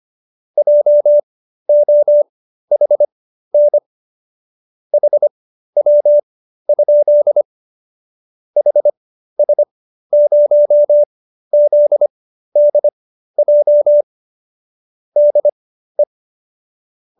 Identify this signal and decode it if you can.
Morse code, heard as a keyed tone, about 25 wpm, spaced as for 13 wpm, 590 Hz. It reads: JOHN HW? HS0ZDJ DE